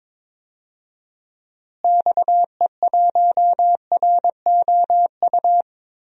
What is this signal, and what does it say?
Morse code, heard as a keyed tone, about 22 wpm, 700 Hz. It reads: XE1ROU
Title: DXCC-100 Call Signs - 6 Characters - Repeated 3x - 22wpm